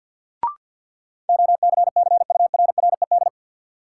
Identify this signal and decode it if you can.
Morse code, heard as a keyed tone, about 50 wpm, 700 Hz. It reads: OCCURRED
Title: Sets of 5 Words Encore - New Word Drill - 50wpm